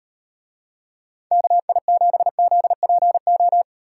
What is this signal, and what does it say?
Morse code, heard as a keyed tone, about 38 wpm, 700 Hz. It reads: KI7ZPO